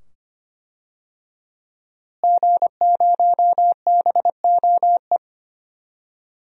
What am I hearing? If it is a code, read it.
Morse code, heard as a keyed tone, about 25 wpm, 700 Hz. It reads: G0BOE
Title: DXCC-100 Call Signs - 5 Characters - Repeated 2x - 25wpm